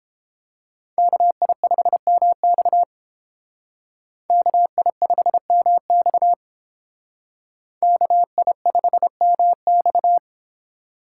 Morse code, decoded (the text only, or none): KI5MX KI5MX KI5MX